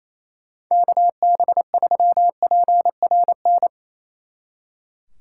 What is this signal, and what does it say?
Morse code, heard as a keyed tone, about 34 wpm, 700 Hz. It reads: KB3PRN
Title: Speed Racing - USA 2x3 Call Signs 28wpm